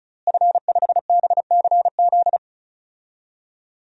Morse code, heard as a keyed tone, about 36 wpm, 700 Hz. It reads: F5BCZ
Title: DXCC-100 Call Signs - 5 Characters - Repeated 3x - 35wpm